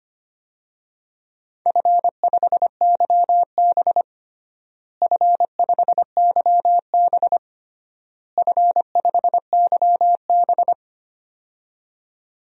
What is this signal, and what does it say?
Morse code, heard as a keyed tone, about 25 wpm, 700 Hz. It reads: F5YB F5YB F5YB
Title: DXCC-100 Call Signs - 4 Characters - Repeated 3x - 25wpm